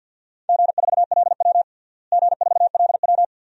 Morse code, heard as a keyed tone, about 52 wpm, 700 Hz. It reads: G4LW G4LW
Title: DXCC-100 Call Signs - 4 Characters - Repeated 2x - 50wpm